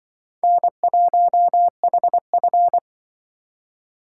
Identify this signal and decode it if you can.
Morse code, heard as a keyed tone, about 23 wpm, 700 Hz. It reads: N1HF
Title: Speed Racing - USA 1x2 Call Signs 20wpm